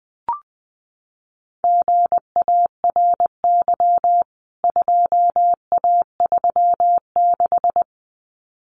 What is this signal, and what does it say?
Morse code, heard as a keyed tone, about 20 wpm, 700 Hz. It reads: GARY 2A36
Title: CWT Exchange 20wpm